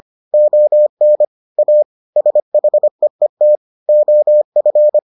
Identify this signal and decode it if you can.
Morse code, heard as a keyed tone, about 25 wpm, 600 Hz. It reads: ON A SHEET OF